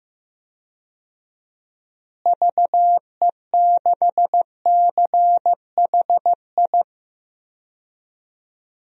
Morse code, heard as a keyed tone, about 15 wpm, 700 Hz. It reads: VE6CHI
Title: DXCC-100 Call Signs - 6 Characters - Repeated 2x - 15wpm